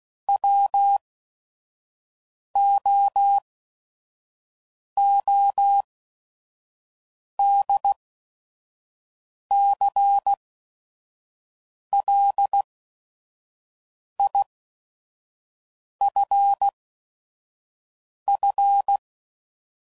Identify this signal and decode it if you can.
Morse code, heard as a keyed tone, about 16 wpm, spaced as for 5 wpm, 800 Hz. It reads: WOODCLIFF
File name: CW Code Groups @5WPM / fromSM's / Cities